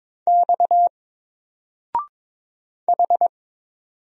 Morse code, heard as a keyed tone, about 22 wpm, 700 Hz. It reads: X H
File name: Single-Letter-Number-22wpm